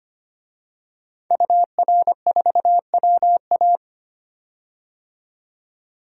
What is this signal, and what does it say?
Morse code, heard as a keyed tone, about 25 wpm, 700 Hz. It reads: UR4WA
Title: DXCC-100 Call Signs - 5 Characters - 25wpm